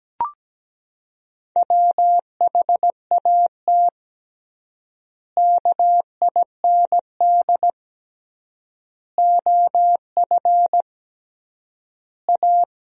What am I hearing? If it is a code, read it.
Morse code, heard as a keyed tone, about 17 wpm, 700 Hz. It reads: WHAT KIND OF A
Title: 3x Word-Spacing - Sentences from Top 400 Words - 17wpm